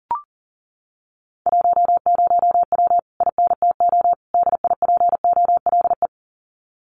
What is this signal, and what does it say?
Morse code, heard as a keyed tone, about 40 wpm, 700 Hz. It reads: 10W INTO DIPOLE
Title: Common QSO Phrases Encore 40wpm